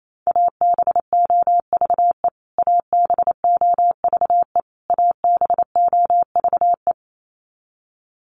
Morse code, heard as a keyed tone, about 28 wpm, 700 Hz. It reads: ABOVE ABOVE ABOVE
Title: Top 500 Words Repeated 28wpm